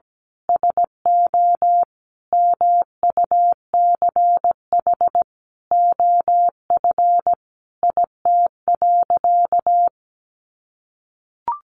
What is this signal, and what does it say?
Morse code, heard as a keyed tone, about 17 wpm, 700 Hz. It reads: SO MUCH OF IT.
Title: Sentences from Top 1400 Words 17wpm